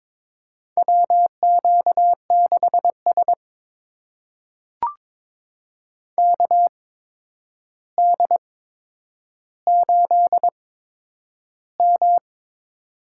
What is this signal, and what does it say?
Morse code, heard as a keyed tone, about 22 wpm, spaced as for 9 wpm, 700 Hz. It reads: WQ6S KD8M
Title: USA Call Signs - 4 Characters - 8x Character Spacing - 22wpm